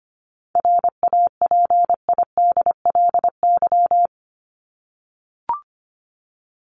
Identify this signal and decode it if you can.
Morse code, heard as a keyed tone, about 25 wpm, 700 Hz. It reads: RAPIDLY
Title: LY Suffix - Top 100 Words with Base Words 25wpm